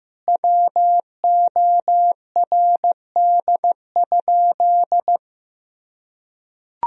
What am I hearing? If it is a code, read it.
Morse code, heard as a keyed tone, about 15 wpm, 700 Hz. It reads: WORD?
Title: Sentences from Top 1000 Words - Review Part 4 - 15wpm